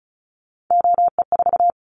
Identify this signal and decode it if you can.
Morse code, heard as a keyed tone, about 36 wpm, 700 Hz. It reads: OE4
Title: Callsign Prefixes for Top 25 DXCC - 35wpm